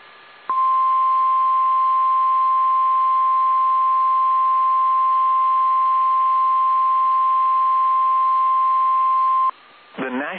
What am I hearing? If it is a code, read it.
Emergency Alert System (SAME) tones (the 1050 Hz attention tone). It is heard from 0.4 to 9.5 s.